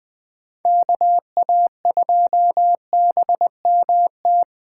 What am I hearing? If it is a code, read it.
Morse code, heard as a keyed tone, about 20 wpm, 700 Hz. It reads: KA2BMT
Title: Speed Racing - USA 2x3 Call Signs 17wpm